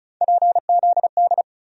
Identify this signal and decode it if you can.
Morse code, heard as a keyed tone, about 34 wpm, 700 Hz. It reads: PZD
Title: USA Call Sign Postfixes - 3 Letters 35wpm